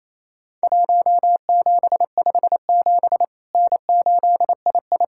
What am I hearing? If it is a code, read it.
Morse code, heard as a keyed tone, about 28 wpm, 700 Hz. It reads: 1757 N8II
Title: Worked All Europe DX Contest - QTC Exchange - 28wpm